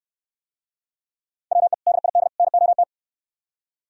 Morse code, heard as a keyed tone, about 69 wpm, 700 Hz. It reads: WE DID NOT